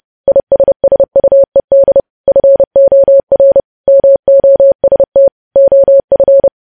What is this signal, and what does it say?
Morse code, heard as a keyed tone, about 30 wpm, 550 Hz. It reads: ISSUED FOR MOST OF
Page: news headlines in morse code at 30 wpm